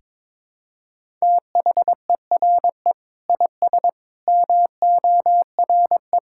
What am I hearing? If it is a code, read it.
Morse code, heard as a keyed tone, about 22 wpm, 700 Hz. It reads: THERE IS MORE
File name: Sentences with 9-Word Phrases - 22wpm